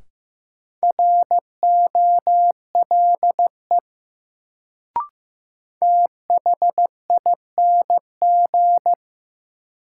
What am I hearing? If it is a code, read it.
Morse code, heard as a keyed tone, about 15 wpm, 700 Hz. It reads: ROLE THING
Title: Sets of 3 Words - New Word Drill - 15wpm